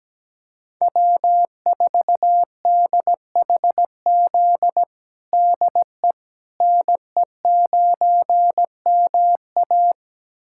Morse code, heard as a keyed tone, about 17 wpm, 700 Hz. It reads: W4DHZ DE NE9MA
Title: USA Callsign DE Callsign - 5 Characters - 17wpm